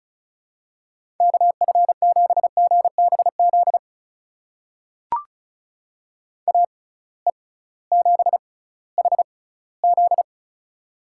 Morse code, heard as a keyed tone, about 36 wpm, spaced as for 19 wpm, 700 Hz. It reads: KF7GBZ AE7HZ
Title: USA Call Signs - 6x Character Spacing - 35wpm